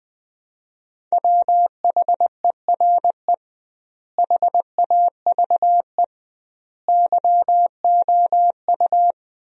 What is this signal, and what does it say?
Morse code, heard as a keyed tone, about 20 wpm, 700 Hz. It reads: WHERE HAVE YOU
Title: 2x Word-Spacing - Sentences from Top 300 Words - 20wpm